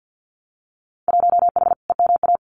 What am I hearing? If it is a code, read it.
Morse code, heard as a keyed tone, about 50 wpm, 700 Hz. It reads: 1H EMA
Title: Winter Field Day Exchanges 50wpm